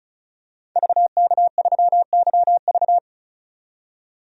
Morse code, heard as a keyed tone, about 34 wpm, 700 Hz. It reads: VK3YV
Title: DXCC-100 Call Signs - 5 Characters - 35wpm